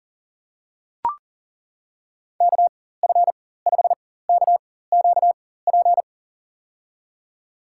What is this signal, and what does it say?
Morse code, heard as a keyed tone, about 40 wpm, spaced as for 19 wpm, 700 Hz. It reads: KF5KQP
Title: USA Call Signs - 6 Characters - 4x Character Spacing - 40wpm